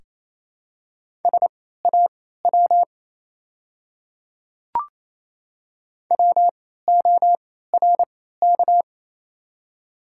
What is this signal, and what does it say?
Morse code, heard as a keyed tone, about 29 wpm, spaced as for 16 wpm, 700 Hz. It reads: SAW WORK
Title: 3x Character Spacing - Top 300 Words - 28wpm